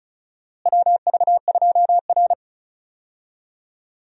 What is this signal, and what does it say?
Morse code, heard as a keyed tone, about 36 wpm, 700 Hz. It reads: WV2R
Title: USA Call Signs - 4 Characters - Repeated 2x - 35wpm